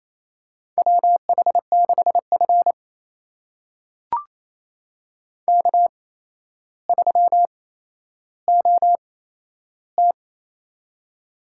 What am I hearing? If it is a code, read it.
Morse code, heard as a keyed tone, about 28 wpm, spaced as for 12 wpm, 700 Hz. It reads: WH6F K3OT
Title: USA Call Signs - 8x Character Spacing - 28wpm